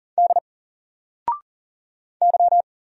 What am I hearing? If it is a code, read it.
Morse code, heard as a keyed tone, about 40 wpm, 700 Hz. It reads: D Y
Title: Single Character (Letter:Number:Punctuation:Prosign) 40wpm